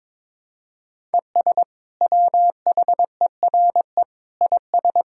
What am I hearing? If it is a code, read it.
Morse code, heard as a keyed tone, about 21 wpm, 700 Hz. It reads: ES WHERE IS
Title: Sentences from Top 200 Words and Abbreviations 22wpm